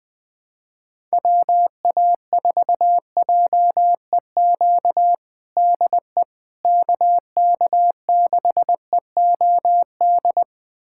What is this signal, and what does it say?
Morse code, heard as a keyed tone, about 20 wpm, 700 Hz. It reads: WA4JEQ DE KK6EOD
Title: USA Callsign DE Callsign 20wpm